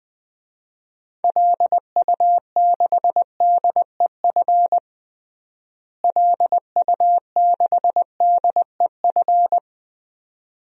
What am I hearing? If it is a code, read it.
Morse code, heard as a keyed tone, about 20 wpm, 700 Hz. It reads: LU6DEF LU6DEF
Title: DXCC-100 Call Signs - 6 Characters - Repeated 2x - 20wpm